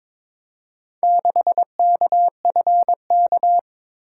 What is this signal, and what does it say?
Morse code, heard as a keyed tone, about 22 wpm, 700 Hz. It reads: 6KFK
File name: practice / 22wpm / Four Letter-Number Combinations - 22wpm